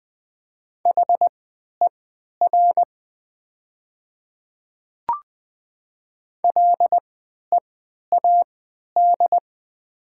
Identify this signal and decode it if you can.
Morse code, heard as a keyed tone, about 20 wpm, spaced as for 11 wpm, 700 Hz. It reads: HER LEAD
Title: 3x Character Spacing - Top 500 Words - 20wpm